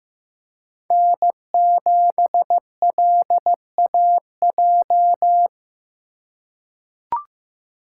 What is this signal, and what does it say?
Morse code, heard as a keyed tone, about 15 wpm, 700 Hz. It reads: N7LAJ
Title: USA Call Signs - 5 Characters - 3x Character Spacing - 15wpm